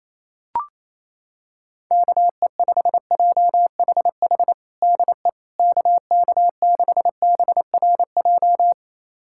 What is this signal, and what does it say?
Morse code, heard as a keyed tone, about 28 wpm, 700 Hz. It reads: KE5JHH DE KK6BRJ
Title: USA Callsign DE Callsign - 6 Characters - 28wpm